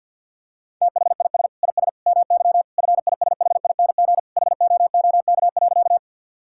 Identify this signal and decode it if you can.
Morse code, heard as a keyed tone, about 50 wpm, 700 Hz. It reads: THIS IS MY FISHING HOOK.
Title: Sentences from Top 1600 Words 50wpm